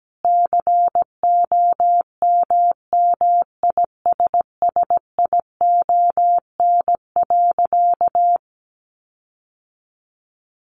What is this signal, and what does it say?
Morse code, heard as a keyed tone, about 17 wpm, 700 Hz. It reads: COMMISSION.
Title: Sentences from Top 1400 Words 17wpm